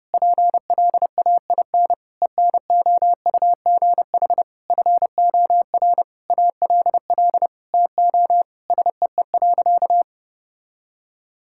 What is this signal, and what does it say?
Morse code, heard as a keyed tone, about 30 wpm, 700 Hz. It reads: PLAIN ENOUGH FOR ALL TO SEE.
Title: Sentences from Top 1000 Words - Review Part 5 - 30wpm